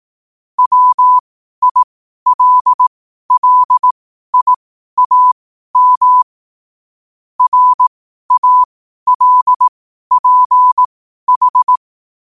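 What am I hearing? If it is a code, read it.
Morse code, heard as a keyed tone, about 18 wpm, spaced as for 12 wpm, 1000 Hz. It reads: WILLIAM RALPH
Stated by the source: Morse code quote of the day at 12 wpm.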